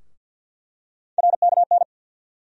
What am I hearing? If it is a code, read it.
Morse code, heard as a keyed tone, about 47 wpm, 700 Hz. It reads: RKN
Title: USA Call Sign Postfixes - 3 Letters 50wpm